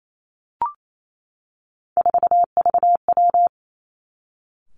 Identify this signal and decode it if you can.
Morse code, heard as a keyed tone, about 28 wpm, 700 Hz. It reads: 4VW
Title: Three Challenging Letter-Number Combinations 28wpm